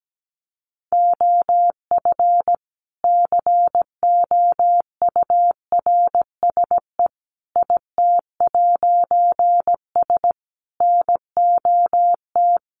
Morse code, heard as a keyed tone, about 17 wpm, 700 Hz. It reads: OF COURSE IT'S NOT
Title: A Sound Of Thunder 17wpm